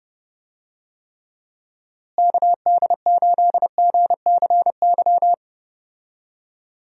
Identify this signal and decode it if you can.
Morse code, heard as a keyed tone, about 30 wpm, 700 Hz. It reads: KD8GCY
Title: USA Call Signs - 6 Characters - Repeated 3x - 30wpm